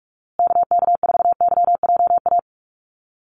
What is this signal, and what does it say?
Morse code, heard as a keyed tone, about 44 wpm, 700 Hz. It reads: KK4YJA
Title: USA Call Signs Repeated 2x 45wpm